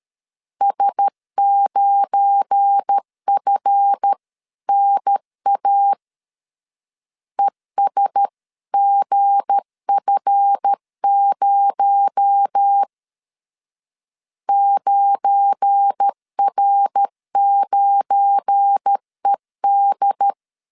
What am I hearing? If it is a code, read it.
Morse code, heard as a keyed tone, about 13 wpm, 800 Hz. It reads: S9FNA ESGF0 9R9ED
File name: LEZIONI / 7 GF9Z / 7 GF9 12wpm 2319